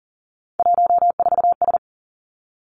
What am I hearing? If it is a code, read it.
Morse code, heard as a keyed tone, about 40 wpm, 700 Hz. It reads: 14S
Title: Three Challenging Letter-Number Combinations 40wpm